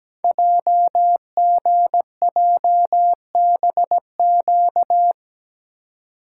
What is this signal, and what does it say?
Morse code, heard as a keyed tone, about 17 wpm, 700 Hz. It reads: JGJBQ
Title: Code Groups - 5 Letters - 17wpm